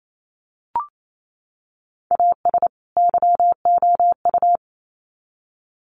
Morse code, heard as a keyed tone, about 28 wpm, 700 Hz. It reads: AS YOU
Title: Sets of 2 Words 28wpm